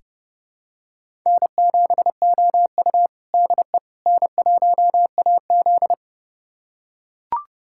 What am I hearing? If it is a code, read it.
Morse code, heard as a keyed tone, about 32 wpm, 700 Hz. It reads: N7OU DE N1AZ
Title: USA Callsign DE Callsign - 4 Characters - 30wpm